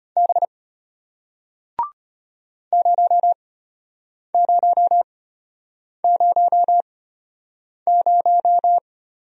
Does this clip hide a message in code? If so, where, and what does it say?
Yes: Morse code from 0.2 to 8.8 s, heard as a keyed tone, about 33 wpm, spaced as for 8 wpm, 700 Hz. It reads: B 0000